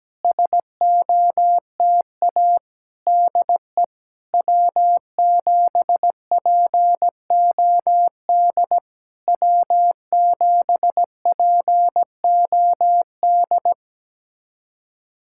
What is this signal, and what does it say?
Morse code, heard as a keyed tone, about 17 wpm, 700 Hz. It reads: SOTA DE W7POD W7POD
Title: SOTA QSO Practice 17wpm